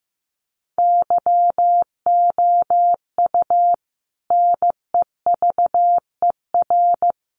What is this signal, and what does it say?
Morse code, heard as a keyed tone, about 14 wpm, 700 Hz. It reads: YOU NEVER
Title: Sentences from Top 1500 Words 15wpm